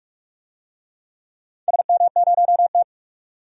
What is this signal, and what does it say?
Morse code, heard as a keyed tone, about 43 wpm, 700 Hz. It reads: SM0T
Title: DXCC-100 Call Signs - 4 Characters - 45wpm